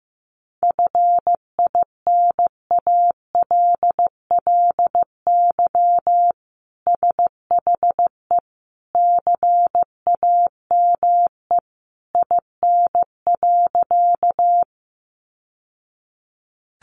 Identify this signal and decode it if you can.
Morse code, heard as a keyed tone, about 15 wpm, 700 Hz. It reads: FINALLY SHE CAME IN.
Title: Sentences from Top 1000 Words 15wpm